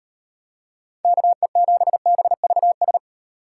Morse code, heard as a keyed tone, about 36 wpm, 700 Hz. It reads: KE7BVS